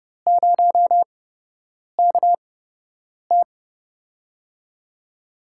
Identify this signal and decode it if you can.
Morse code, heard as a keyed tone, about 30 wpm, spaced as for 8 wpm, 700 Hz. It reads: 0KT
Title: USA Call Signs - 8x Character Spacing - 30wpm